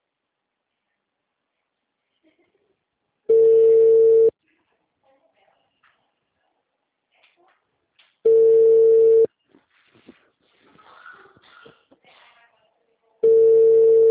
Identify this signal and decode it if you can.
Ringback tone (China, 450 Hz), with speech.
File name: out- -2014- - - .47779